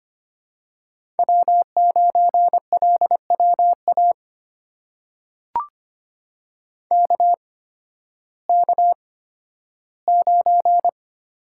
Morse code, heard as a keyed tone, about 25 wpm, spaced as for 13 wpm, 700 Hz. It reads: W9LWA KK9
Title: USA Call Signs - 8x Character Spacing - 25wpm